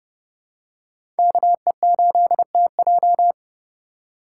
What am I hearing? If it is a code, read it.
Morse code, heard as a keyed tone, about 30 wpm, 700 Hz. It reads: KE8TJ
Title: USA Call Signs - 5 Characters - 3x Character Spacing - 30wpm